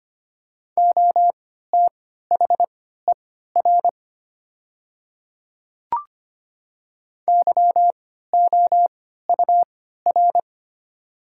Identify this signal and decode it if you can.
Morse code, heard as a keyed tone, about 25 wpm, spaced as for 14 wpm, 700 Hz. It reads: OTHER YOUR
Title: 3x Character Spacing - Top 100 Words - 25wpm